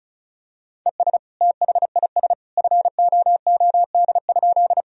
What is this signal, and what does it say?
Morse code, heard as a keyed tone, about 34 wpm, 700 Hz. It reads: ES THIS FOOD?
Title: Sentences from Top 300 Words and Abbreviations 35wpm